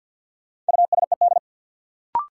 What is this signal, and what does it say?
Morse code, heard as a keyed tone, about 53 wpm, 700 Hz. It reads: USED
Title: Sentences from Top 600 Words - New Word Drill - 50wpm